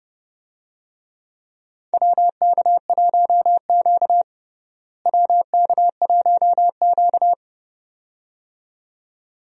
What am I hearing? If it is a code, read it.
Morse code, heard as a keyed tone, about 32 wpm, 700 Hz. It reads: WK1Q WK1Q
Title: USA Call Signs - 4 Characters - Repeated 2x - 30wpm